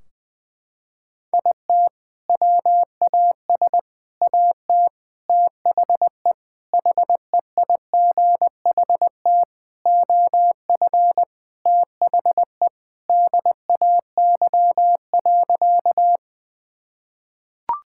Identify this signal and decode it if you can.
Morse code, heard as a keyed tone, about 21 wpm, 700 Hz. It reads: IT WAS AT THE HEIGHT OF THE DAY.
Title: Sentences from Top 1600 Words 20wpm